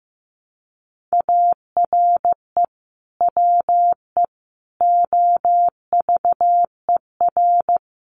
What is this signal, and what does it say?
Morse code, heard as a keyed tone, about 15 wpm, 700 Hz. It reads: ARE WE OVER
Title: Sentences from Top 1200 Words 15wpm